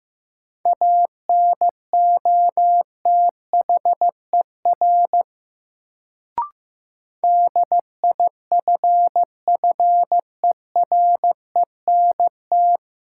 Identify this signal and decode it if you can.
Morse code, heard as a keyed tone, about 15 wpm, 700 Hz. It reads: ANOTHER DIFFERENT
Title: Sentences from Top 600 Words - New Word Drill - 15wpm